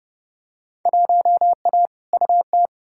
Morse code, heard as a keyed tone, about 31 wpm, 700 Hz. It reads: 1A UT
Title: Field Day Exchanges - 30wpm